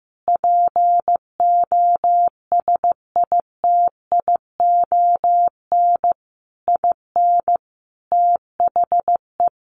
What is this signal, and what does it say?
Morse code, heard as a keyed tone, about 14 wpm, 700 Hz. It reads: POSITION IN THE
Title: Sentences from Top 900 Words 15wpm